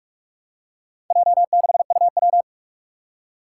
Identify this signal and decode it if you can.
Morse code, heard as a keyed tone, about 46 wpm, 700 Hz. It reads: J6UW
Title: Four Challenging Letter-Number Combinations 45wpm